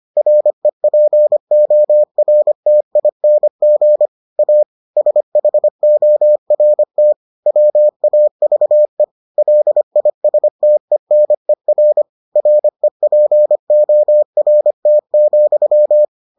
Morse code, heard as a keyed tone, about 25 wpm, 600 Hz. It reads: REPORTING A SHORT WAVE LISTENER REPORT,